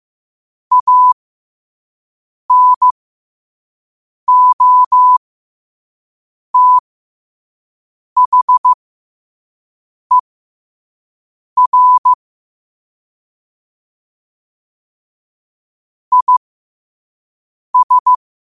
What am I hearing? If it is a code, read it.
Morse code, heard as a keyed tone, about 15 wpm, spaced as for 5 wpm, 1000 Hz. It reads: ANOTHER IS
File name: qotdfiles / 5wpm